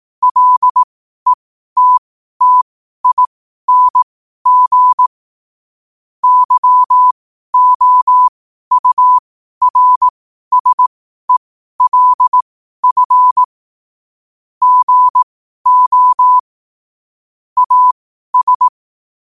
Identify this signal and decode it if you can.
Morse code, heard as a keyed tone, about 18 wpm, spaced as for 12 wpm, 1000 Hz. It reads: LETTING YOURSELF GO AS
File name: qotdfiles / 12wpm